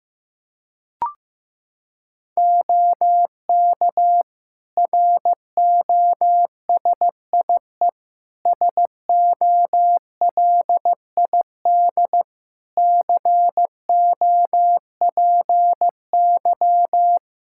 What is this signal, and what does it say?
Morse code, heard as a keyed tone, about 15 wpm, 700 Hz. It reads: OK ROSIE SOLID COPY
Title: Practice QSO - Exchange 3 - More Info - 15wpm